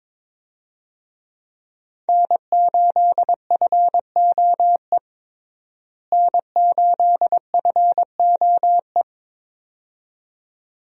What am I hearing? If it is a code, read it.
Morse code, heard as a keyed tone, about 22 wpm, 700 Hz. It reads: N8FOE N8FOE
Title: USA Call Signs - 5 Characters - Repeated 2x - 22wpm